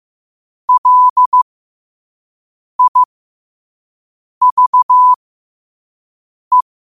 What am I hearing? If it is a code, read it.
Morse code, heard as a keyed tone, about 15 wpm, spaced as for 5 wpm, 1000 Hz. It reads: LIVE
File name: qotdfiles / 5wpm